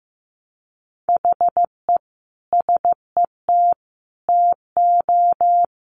Morse code, heard as a keyed tone, about 15 wpm, 700 Hz. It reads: HE SET TO
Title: Sentences from Top 1200 Words 15wpm